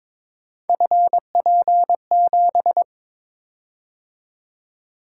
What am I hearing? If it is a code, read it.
Morse code, heard as a keyed tone, about 22 wpm, 700 Hz. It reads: FP7
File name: practice / 22wpm / Three Letter-Number Combinations - 22wpm